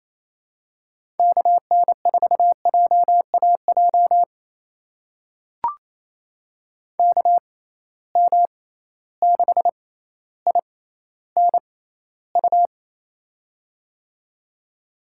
Morse code, heard as a keyed tone, about 28 wpm, spaced as for 14 wpm, 700 Hz. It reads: KN4JAJ KM6INU
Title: USA Call Signs - 6x Character Spacing - 28wpm